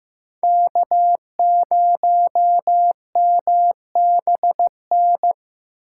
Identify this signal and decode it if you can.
Morse code, heard as a keyed tone, about 18 wpm, 700 Hz. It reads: K0MBN
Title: Speed Racing - USA 1x3 Call Signs 15wpm